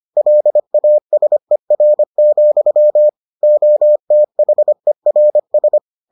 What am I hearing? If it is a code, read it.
Morse code, heard as a keyed tone, about 25 wpm, 600 Hz. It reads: LASER, OTHERS